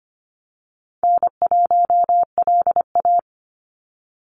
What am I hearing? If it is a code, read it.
Morse code, heard as a keyed tone, about 25 wpm, 700 Hz. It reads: N1LA
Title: USA Call Signs 25wpm